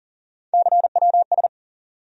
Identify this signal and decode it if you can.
Morse code, heard as a keyed tone, about 40 wpm, 700 Hz. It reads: CWS
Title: USA Call Sign Postfixes - 3 Letters 40wpm